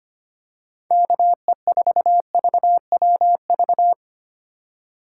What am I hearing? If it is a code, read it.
Morse code, heard as a keyed tone, about 29 wpm, 700 Hz. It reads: KE4VWV
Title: Speed Racing - USA Call Signs 25wpm